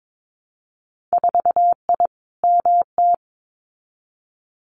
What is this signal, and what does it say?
Morse code, heard as a keyed tone, about 22 wpm, 700 Hz. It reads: 4I MT
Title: Winter Field Day Exchanges Encore 22wpm